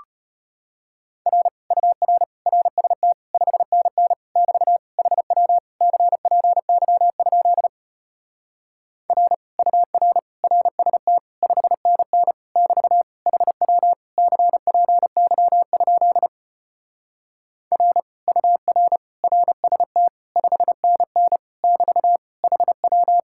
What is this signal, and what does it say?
Morse code, heard as a keyed tone, about 33 wpm, 700 Hz. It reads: R UR RST 5NN <BT> HW CPY? R UR RST 5NN <BT> HW CPY? R UR RST 5NN <BT> HW